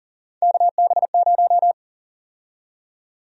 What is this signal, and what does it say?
Morse code, heard as a keyed tone, about 38 wpm, 700 Hz. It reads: KB0
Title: USA Call Sign Prefixes - Repeated 2x - 40wpm